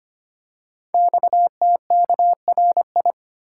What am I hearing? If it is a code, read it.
Morse code, heard as a keyed tone, about 25 wpm, 700 Hz. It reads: XTKRI
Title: Code Groups - 5 Letters - 25wpm